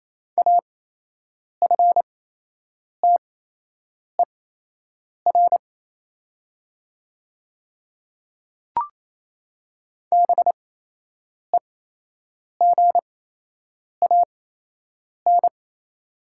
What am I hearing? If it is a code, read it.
Morse code, heard as a keyed tone, about 28 wpm, spaced as for 8 wpm, 700 Hz. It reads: AFTER BEGAN
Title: 8x Character Spacing - Top 500 Words - 28wpm